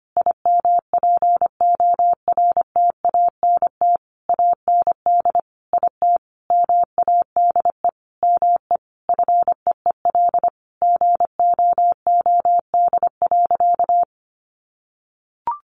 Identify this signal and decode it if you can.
Morse code, heard as a keyed tone, about 25 wpm, 700 Hz. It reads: IMPORTANT AND IT MADE ME FEEL GOOD.
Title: Sentences from Top 600 Words 25wpm